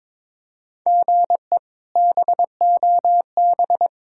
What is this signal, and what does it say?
Morse code, heard as a keyed tone, about 21 wpm, 700 Hz. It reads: GE BOB
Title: K1USN SST QSO Practice - 22wpm